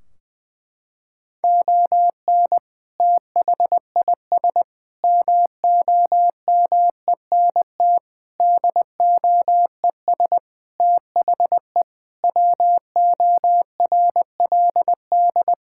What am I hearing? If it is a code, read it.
Morse code, heard as a keyed tone, about 19 wpm, 700 Hz. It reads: ON THIS MOMENT DOES THE WORLD
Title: Sentences from Top 1600 Words 20wpm